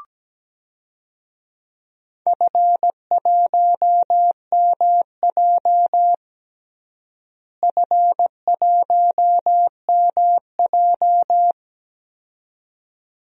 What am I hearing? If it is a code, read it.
Morse code, heard as a keyed tone, about 17 wpm, 700 Hz. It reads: F1MJ F1MJ
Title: DXCC-100 Call Signs - 4 Characters - Repeated 2x - 17wpm